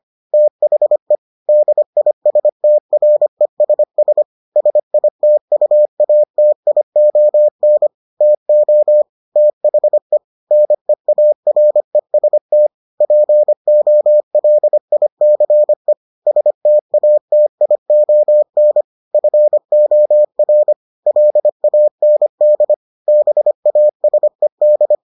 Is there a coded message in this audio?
Yes: Morse code from 0.3 to 24.9 s, heard as a keyed tone, about 25 wpm, 600 Hz. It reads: THE DISTRESS SITUATION TO THE NEAREST POLICE STATION FOR LAND BASED